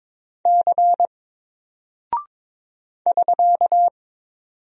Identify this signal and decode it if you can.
Morse code, heard as a keyed tone, about 22 wpm, 700 Hz. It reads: C <SK>
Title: Single Character (Letter:Number:Punctuation:Prosign) Alt 22wpm